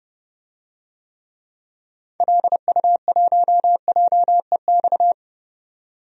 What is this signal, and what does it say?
Morse code, heard as a keyed tone, about 31 wpm, 700 Hz. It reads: LU1JEX
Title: DXCC-100 Call Signs - 6 Characters - 30wpm